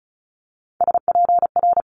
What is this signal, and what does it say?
Morse code, heard as a keyed tone, about 36 wpm, 700 Hz. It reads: SPR
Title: Multi-letter Phonemes Quick Fire 35wpm